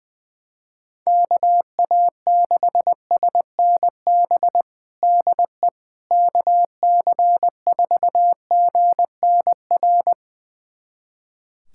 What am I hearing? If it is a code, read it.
Morse code, heard as a keyed tone, about 20 wpm, 700 Hz. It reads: KA6SNB DE KC4GNR
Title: USA Callsign DE Callsign - 6 Characters - 20wpm